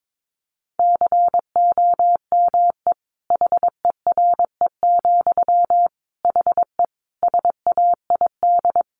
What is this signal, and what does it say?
Morse code, heard as a keyed tone, about 22 wpm, 700 Hz. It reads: COME HERE, HE SAID
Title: Sentences from Top 1500 Words 22wpm